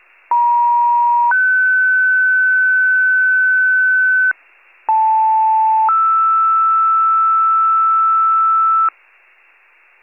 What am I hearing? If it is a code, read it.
Two-tone page, heard as a one-second tone, then a three-second tone, about 950 then 1500 Hz, about 880 then 1320 Hz, with noise. It is heard from 0.3 to 8.9 s.